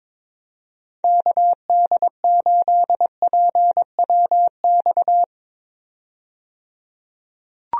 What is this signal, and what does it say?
Morse code, heard as a keyed tone, about 22 wpm, 700 Hz. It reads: KD8PWX
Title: USA Call Signs - 6 Characters - 22wpm